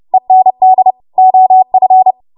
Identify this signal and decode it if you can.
Morse code, heard as a keyed tone, about 30 wpm, 750 Hz. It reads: END OF
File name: Morse / 30WPM